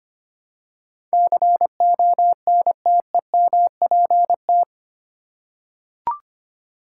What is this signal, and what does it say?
Morse code, heard as a keyed tone, about 25 wpm, 700 Hz. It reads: CONTEMPT
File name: Eight Letter Words - 25wpm